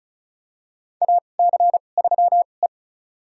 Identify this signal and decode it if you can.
Morse code, heard as a keyed tone, about 36 wpm, spaced as for 25 wpm, 700 Hz. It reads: AC3E
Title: USA Call Signs - 4 Characters - 2x Character Spacing - 35wpm